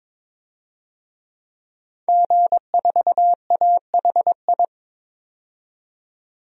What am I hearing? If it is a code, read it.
Morse code, heard as a keyed tone, about 22 wpm, 700 Hz. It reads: G4AHI
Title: DXCC-100 Call Signs - 5 Characters - 22wpm